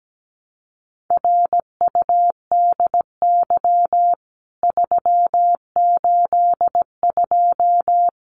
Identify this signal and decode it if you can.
Morse code, heard as a keyed tone, about 17 wpm, 700 Hz. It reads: RUDY 382
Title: CWT Exchange 17wpm